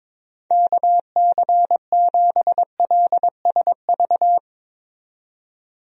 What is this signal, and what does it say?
Morse code, heard as a keyed tone, about 21 wpm, 700 Hz. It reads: KC7LSV
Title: USA Call Signs - 6 Characters - 22wpm